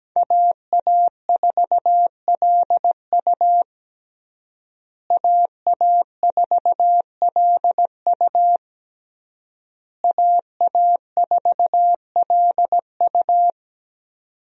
Morse code, heard as a keyed tone, about 17 wpm, 700 Hz. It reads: AA4LU AA4LU AA4LU
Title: USA Call Signs - 5 Characters - Repeated 3x - 17wpm